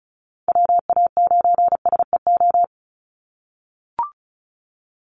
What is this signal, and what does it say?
Morse code, heard as a keyed tone, about 34 wpm, 700 Hz. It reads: WA9SEO
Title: Call Signs - 35wpm